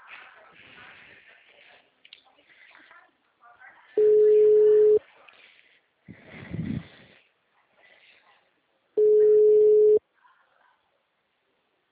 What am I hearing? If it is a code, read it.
Ringback tone (Europe, 425 Hz), with speech.